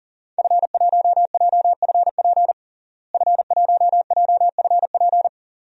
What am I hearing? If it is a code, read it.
Morse code, heard as a keyed tone, about 40 wpm, 700 Hz. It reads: F1JFP F1JFP
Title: DXCC-100 Call Signs - 5 Characters - Repeated 2x - 40wpm